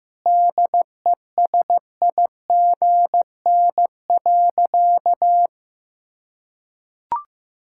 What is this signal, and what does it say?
Morse code, heard as a keyed tone, about 16 wpm, 700 Hz. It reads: DESIGN.
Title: Sentences from Top 1000 Words - Review Part 6 - 15wpm